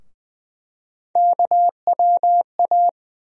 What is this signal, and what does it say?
Morse code, heard as a keyed tone, about 20 wpm, 700 Hz. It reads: KWA
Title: USA Call Sign Postfixes - 3 Letters 20wpm